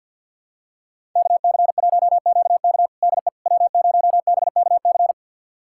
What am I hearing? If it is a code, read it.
Morse code, heard as a keyed tone, about 48 wpm, 700 Hz. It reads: KC1QK DE W0BKC
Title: USA Callsign DE Callsign - 5 Characters - 50wpm